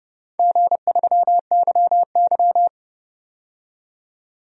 Morse code, heard as a keyed tone, about 31 wpm, 700 Hz. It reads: G3YY
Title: DXCC-100 Call Signs - 4 Characters - 30wpm